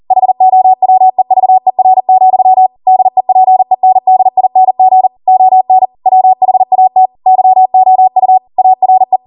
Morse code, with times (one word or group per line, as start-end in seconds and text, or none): 0.1-2.7 HOWEVER,
2.9-5.1 DEPENDING
5.3-5.9 ON
6.1-7.1 WHAT
7.3-8.4 YOU
8.6-9.2 ARE